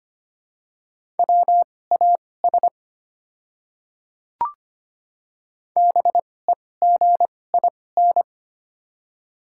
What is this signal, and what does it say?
Morse code, heard as a keyed tone, about 25 wpm, spaced as for 18 wpm, 700 Hz. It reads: WAS BEGIN
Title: 2x Character Spacing - Top 300 Words - 25wpm